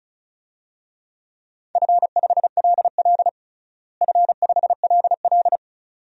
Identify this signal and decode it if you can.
Morse code, heard as a keyed tone, about 36 wpm, 700 Hz. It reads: F5LL F5LL
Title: DXCC-100 Call Signs - 4 Characters - Repeated 2x - 35wpm